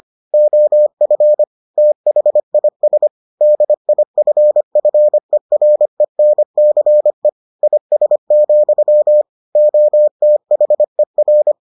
Morse code, heard as a keyed tone, about 25 wpm, 600 Hz. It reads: OF THIS DIFFERENCE IS, OTHER